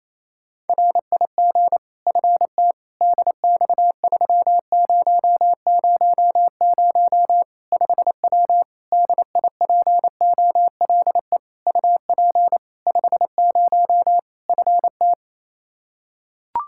Morse code, heard as a keyed tone, about 28 wpm, 700 Hz. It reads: RIG FT DX3000 5W DIPOLE UP 50 FT
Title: Practice QSO - Exchange 2 - Info - 28wpm